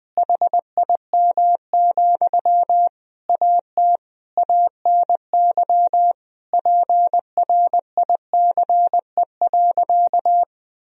Morse code, heard as a keyed tone, about 20 wpm, 700 Hz. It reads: HIM, AT ANY PRICE.
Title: Sentences from Top 1000 Words - Review Part 4 - 20wpm